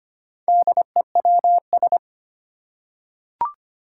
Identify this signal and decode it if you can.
Morse code, heard as a keyed tone, about 25 wpm, 700 Hz. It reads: DEWS
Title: Four Challenging Letter-Number Combinations 25wpm